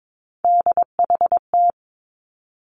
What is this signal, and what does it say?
Morse code, heard as a keyed tone, about 22 wpm, 700 Hz. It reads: DHT
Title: Three Challenging Letter-Number Combinations 22wpm